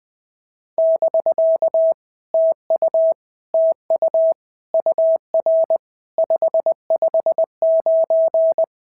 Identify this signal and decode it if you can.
Morse code, heard as a keyed tone, about 20 wpm, 650 Hz. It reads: <BK> TU TU UR 559
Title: POTA QSO Practice - 20wpm